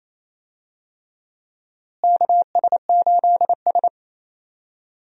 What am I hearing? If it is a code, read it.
Morse code, heard as a keyed tone, about 27 wpm, 700 Hz. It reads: KS8S
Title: USA Call Signs - 4 Characters - Repeated 3x - 28wpm